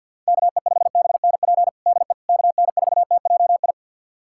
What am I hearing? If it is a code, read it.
Morse code, heard as a keyed tone, about 50 wpm, 700 Hz. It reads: KE5BNP DE KN4TJI
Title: USA Callsign DE Callsign 50wpm